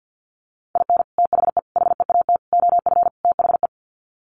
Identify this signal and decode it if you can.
Morse code, heard as a keyed tone, about 50 wpm, 700 Hz. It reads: IN THE HEAT OF THE
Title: Sets of 5 Words Encore2 50wpm